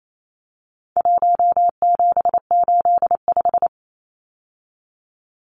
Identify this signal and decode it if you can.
Morse code, heard as a keyed tone, about 28 wpm, 700 Hz. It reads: 1785
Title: Numbers - 4-Digit 28wpm